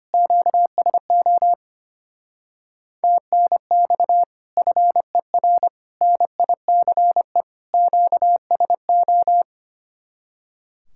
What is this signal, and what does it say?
Morse code, heard as a keyed tone, about 30 wpm, 700 Hz. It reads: QSO TNX FER NICE QSO
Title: Speed-Racing Common QSO Phrases Encore - 25wpm